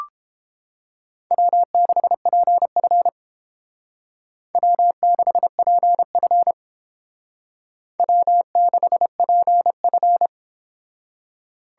Morse code, heard as a keyed tone, about 30 wpm, 700 Hz. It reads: W6PF W6PF W6PF